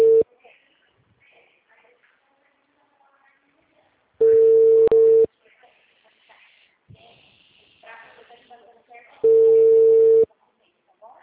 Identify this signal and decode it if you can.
Ringback tone (China, 450 Hz), with speech.